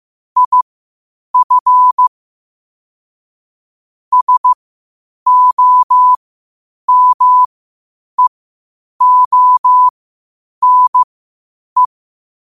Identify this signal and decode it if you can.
Morse code, heard as a keyed tone, about 15 wpm, spaced as for 8 wpm, 1000 Hz. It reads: IF SOMEONE
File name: qotdfiles / 8wpm